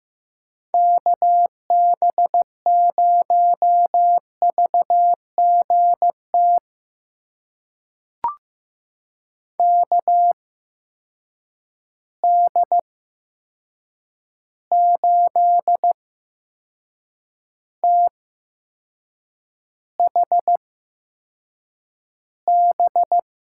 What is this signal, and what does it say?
Morse code, heard as a keyed tone, about 15 wpm, spaced as for 6 wpm, 700 Hz. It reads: KB0VGT KD8THB
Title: USA Call Signs - 6 Characters - 8x Character Spacing - 15wpm